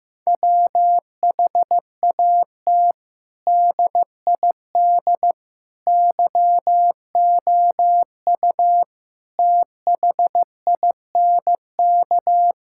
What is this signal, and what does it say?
Morse code, heard as a keyed tone, about 16 wpm, 700 Hz. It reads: WHAT DID YOU THINK
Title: Sentences from Top 1000 Words - Review Part 5 - 15wpm